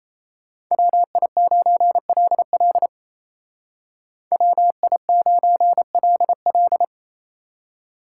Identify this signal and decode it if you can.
Morse code, heard as a keyed tone, about 33 wpm, 700 Hz. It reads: WI9LL WI9LL
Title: Speed Racing - USA Call Signs 28wpm